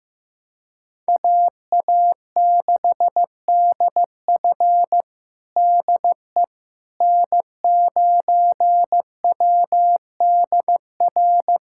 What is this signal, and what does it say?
Morse code, heard as a keyed tone, about 15 wpm, 700 Hz. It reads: AA6DF DE N9WDR
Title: USA Callsign DE Callsign - 5 Characters - 15wpm